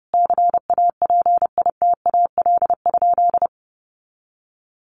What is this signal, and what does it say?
Morse code, heard as a keyed tone, about 30 wpm, 700 Hz. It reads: CAPITAL?
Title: Sentences from Top 1100 Words 30wpm